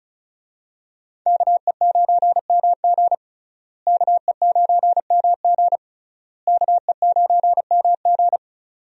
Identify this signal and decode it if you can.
Morse code, heard as a keyed tone, about 36 wpm, 700 Hz. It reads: KE9MG KE9MG KE9MG
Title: USA Call Signs - 5 Characters - Repeated 3x - 35wpm